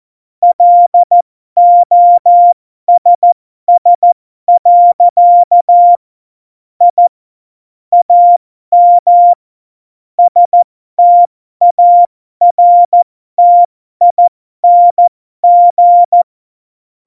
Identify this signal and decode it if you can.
Morse code, heard as a keyed tone, about 14 wpm, spaced as for 12 wpm, 700 Hz. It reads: LOSS. I AM STARTING
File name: Morsekurs / Witze